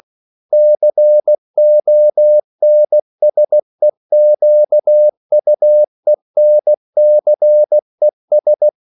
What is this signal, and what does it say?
Morse code, heard as a keyed tone, about 16 wpm, 600 Hz. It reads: CONSEQUENCES
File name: CW / Practice / EnglishQuotations0116